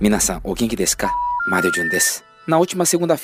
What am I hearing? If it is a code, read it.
Special information tone, with speech.